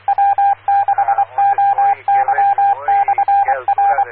Morse code, heard as a keyed tone, about 24 wpm, 800 Hz, with radio noise. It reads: W6OY/R
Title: Claremont, CA Police, Amateur Radio